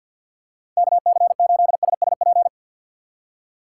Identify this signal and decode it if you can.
Morse code, heard as a keyed tone, about 50 wpm, 700 Hz. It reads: K C 8 S S P
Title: USA Call Signs - 6 Characters - 6x Character Spacing - 50wpm